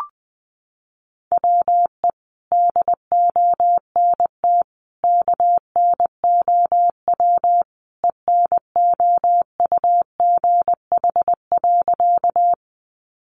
Morse code, heard as a keyed tone, about 20 wpm, 700 Hz. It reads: WE DONT KNOW ENOUGH.
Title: Sentences from Top 400 Words 20wpm